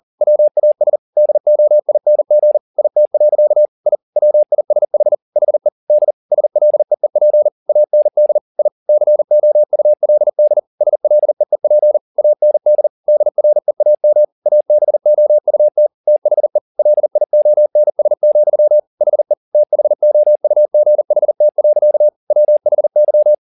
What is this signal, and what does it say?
Morse code, heard as a keyed tone, about 40 wpm, 600 Hz. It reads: WAS DOING IT. I WISH HE D SLEEP AND I COULD SLEEP AND DREAM ABOUT THE LIONS, HE THOUGHT. WHY